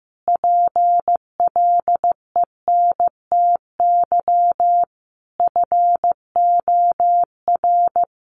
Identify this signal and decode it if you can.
Morse code, heard as a keyed tone, about 15 wpm, 700 Hz. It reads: PLENTY FOR
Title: Sentences from Top 1300 Words 15wpm